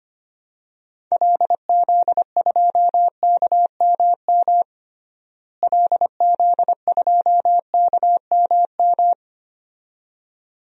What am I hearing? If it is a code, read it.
Morse code, heard as a keyed tone, about 25 wpm, 700 Hz. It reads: LZ2KMM LZ2KMM
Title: DXCC-100 Call Signs - 6 Characters - Repeated 2x - 25wpm